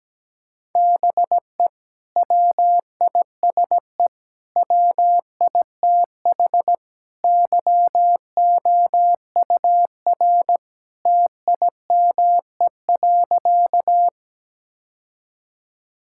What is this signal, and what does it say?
Morse code, heard as a keyed tone, about 17 wpm, 700 Hz. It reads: BE WISE WITH YOUR TIME.
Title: Sentences from Top 1600 Words 17wpm